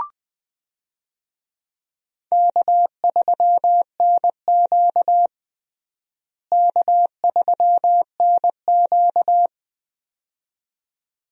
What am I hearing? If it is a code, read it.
Morse code, heard as a keyed tone, about 21 wpm, 700 Hz. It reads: K3NQ K3NQ
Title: USA Call Signs - 4 Characters - Repeated 2x - 20wpm